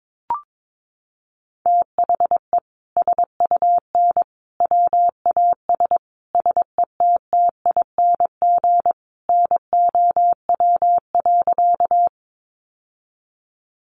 Morse code, heard as a keyed tone, about 22 wpm, 700 Hz. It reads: THE SUN WAS SETTING NOW.
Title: Sentences from Top 1300 Words 22wpm